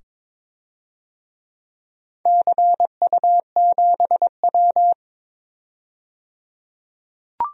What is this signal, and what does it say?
Morse code, heard as a keyed tone, about 22 wpm, 700 Hz. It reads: CU7W
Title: DXCC-100 Call Signs - 4 Characters - 22wpm